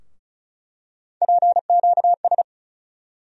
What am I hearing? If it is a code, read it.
Morse code, heard as a keyed tone, about 36 wpm, 700 Hz. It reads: PQS
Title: USA Call Sign Postfixes - 3 Letters 35wpm